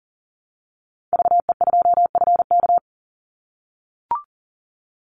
Morse code, heard as a keyed tone, about 41 wpm, 700 Hz. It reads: VE2FK
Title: Call Signs - 40wpm